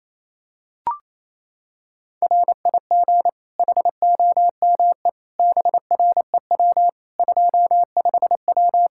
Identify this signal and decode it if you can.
Morse code, heard as a keyed tone, about 28 wpm, 700 Hz. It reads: RIG HOME BREW 25W
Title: K1USN SST QSO Practice 28wpm